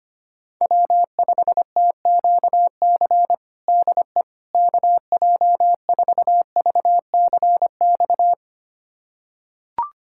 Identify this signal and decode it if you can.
Morse code, heard as a keyed tone, about 25 wpm, 700 Hz. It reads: W5TQC DE KJ4VCX
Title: USA Callsign DE Callsign 25wpm